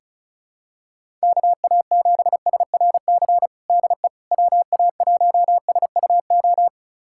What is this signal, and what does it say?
Morse code, heard as a keyed tone, about 34 wpm, 700 Hz. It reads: KA7SRC DE WA1SUO
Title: USA Callsign DE Callsign - 6 Characters - 35wpm